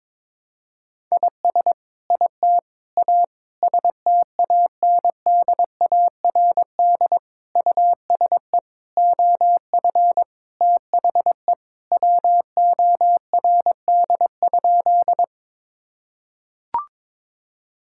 Morse code, heard as a keyed tone, about 22 wpm, 700 Hz. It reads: IS IT A STANDARD USE OF THE WORD?
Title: Sentences from Top 1000 Words - Review Part 6 - 22wpm